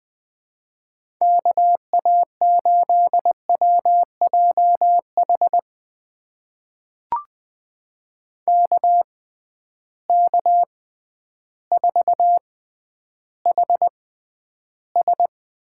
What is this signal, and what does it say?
Morse code, heard as a keyed tone, about 20 wpm, spaced as for 11 wpm, 700 Hz. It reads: KA8WJH KK4HS
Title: USA Call Signs - 6 Characters - 6x Character Spacing - 20wpm